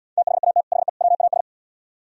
Morse code, heard as a keyed tone, about 76 wpm, 700 Hz. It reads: THAT HE WAS